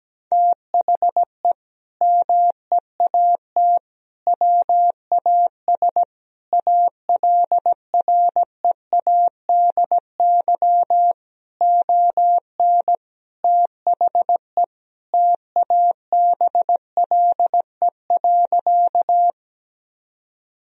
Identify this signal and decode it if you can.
Morse code, heard as a keyed tone, about 17 wpm, 700 Hz. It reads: THE MEAT WAS ALREADY ON THE TABLE.
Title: Sentences from Top 1000 Words - Review Part 3 - 17wpm